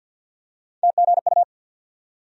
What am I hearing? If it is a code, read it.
Morse code, heard as a keyed tone, about 53 wpm, 700 Hz. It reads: TGU
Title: USA Call Sign Postfixes - 3 Letters 50wpm